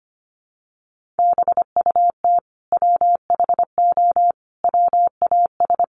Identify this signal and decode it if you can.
Morse code, heard as a keyed tone, about 25 wpm, 700 Hz. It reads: BUT WHO WAS
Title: Sentences from Top 200 Words 25wpm